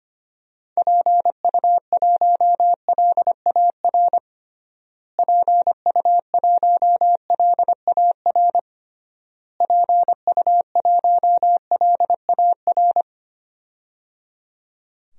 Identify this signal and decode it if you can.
Morse code, heard as a keyed tone, about 25 wpm, 700 Hz. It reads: PU1LAR PU1LAR PU1LAR
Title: DXCC-100 Call Signs - 6 Characters - Repeated 3x - 25wpm